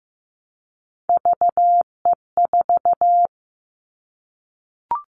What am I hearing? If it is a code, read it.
Morse code, heard as a keyed tone, about 15 wpm, 700 Hz. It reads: VE4
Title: Callsign Prefixes for Top 100 DXCC - 15wpm